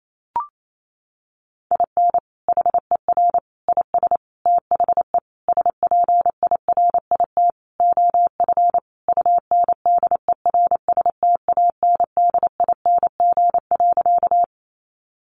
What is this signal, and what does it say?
Morse code, heard as a keyed tone, about 28 wpm, 700 Hz. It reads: IN HER IS THE SPIRIT OF UNDERSTANDING.
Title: Sentences from Top 1400 Words 28wpm